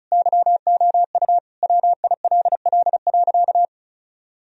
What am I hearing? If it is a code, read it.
Morse code, heard as a keyed tone, about 34 wpm, 700 Hz. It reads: YOU WILL.
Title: Sentences from Top 1000 Words - Review Part 2 - 35wpm